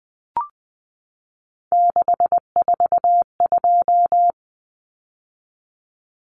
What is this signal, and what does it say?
Morse code, heard as a keyed tone, about 20 wpm, 700 Hz. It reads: 642
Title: Numbers - 3-Digit 20wpm